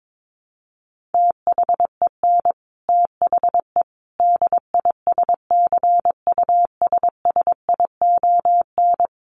Morse code, heard as a keyed tone, about 22 wpm, 700 Hz. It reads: THEN THE DISCUSSION
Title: Sentences from Top 1100 Words 22wpm